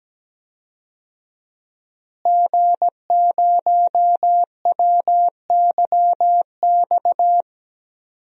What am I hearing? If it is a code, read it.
Morse code, heard as a keyed tone, about 17 wpm, 700 Hz. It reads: G0WYX
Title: DXCC-100 Call Signs - 5 Characters - Repeated 2x - 17wpm